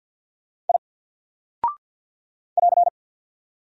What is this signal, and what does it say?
Morse code, heard as a keyed tone, about 49 wpm, 700 Hz. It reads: I <AR>
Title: Single Character (Letter:Number:Punctuation:Prosign) Alt 50wpm